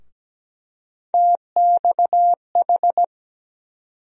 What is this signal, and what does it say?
Morse code, heard as a keyed tone, about 17 wpm, 700 Hz. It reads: TXH
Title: USA Call Sign Postfixes - 3 Letters 17wpm